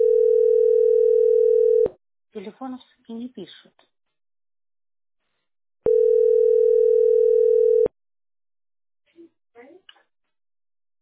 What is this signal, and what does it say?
Ringback tone (North America), 440 + 480 Hz, with speech.
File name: da73ee10-cc1c-4194-83d3